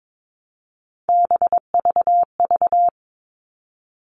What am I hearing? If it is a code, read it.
Morse code, heard as a keyed tone, about 22 wpm, 700 Hz. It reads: BVV
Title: All Cut Numbers - 3-Digit 22wpm